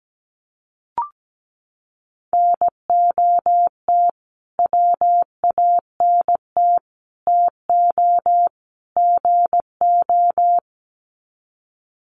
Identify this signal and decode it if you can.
Morse code, heard as a keyed tone, about 17 wpm, 700 Hz. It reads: NOT WANT TO GO
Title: Sets of 4 Words 17wpm